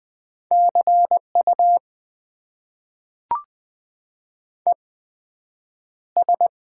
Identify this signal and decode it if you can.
Morse code, heard as a keyed tone, about 20 wpm, spaced as for 9 wpm, 700 Hz. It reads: CU ES
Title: Top 90 QSO Elements - 8x Character Spacing - 20wpm